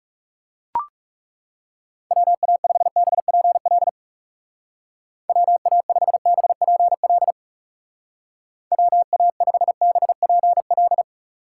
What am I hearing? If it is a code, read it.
Morse code, heard as a keyed tone, about 36 wpm, 700 Hz. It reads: WA5BPL WA5BPL WA5BPL
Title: Speed Racing - USA Call Signs 30wpm